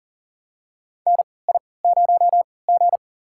Morse code, heard as a keyed tone, about 40 wpm, spaced as for 23 wpm, 700 Hz. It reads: NI0G
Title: USA Call Signs - 3x Character Spacing - 40wpm